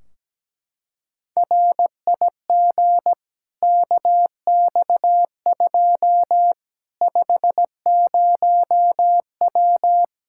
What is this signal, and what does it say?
Morse code, heard as a keyed tone, about 17 wpm, 700 Hz. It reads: RIG KX2 50W
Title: K1USN SST QSO Practice 17wpm